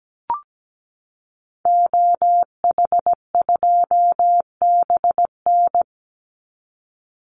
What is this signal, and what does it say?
Morse code, heard as a keyed tone, about 17 wpm, 700 Hz. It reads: OH2BN
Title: Call Signs Repeated 2x - 17wpm